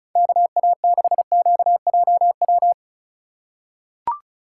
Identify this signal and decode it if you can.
Morse code, heard as a keyed tone, about 36 wpm, 700 Hz. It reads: KA6QJW
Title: USA Call Signs - 6 Characters - 4x Character Spacing - 35wpm